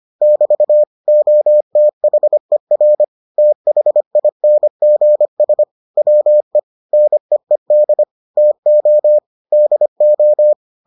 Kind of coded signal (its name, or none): Morse code